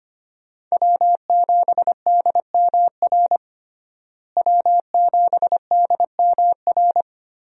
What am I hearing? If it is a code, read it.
Morse code, heard as a keyed tone, about 25 wpm, 700 Hz. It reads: W7DMR W7DMR
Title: USA Call Signs - 5 Characters - Repeated 3x - 25wpm